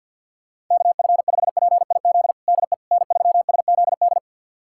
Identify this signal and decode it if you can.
Morse code, heard as a keyed tone, about 52 wpm, 700 Hz. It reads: KF5PIZ DE N3SZD
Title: USA Callsign DE Callsign 50wpm